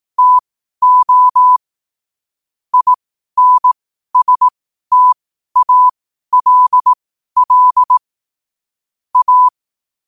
Morse code, heard as a keyed tone, about 18 wpm, spaced as for 12 wpm, 1000 Hz. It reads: TO INSTALL A
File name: qotdfiles / 12wpm